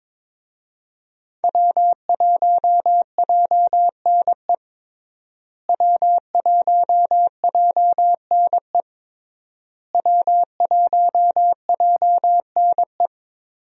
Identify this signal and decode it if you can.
Morse code, heard as a keyed tone, about 22 wpm, 700 Hz. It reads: W1JNE W1JNE W1JNE
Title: USA Call Signs - 5 Characters - Repeated 3x - 22wpm